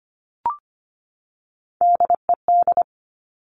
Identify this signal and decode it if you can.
Morse code, heard as a keyed tone, about 25 wpm, 700 Hz. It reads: DED
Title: Top Three Letter Combinations in Words 25wpm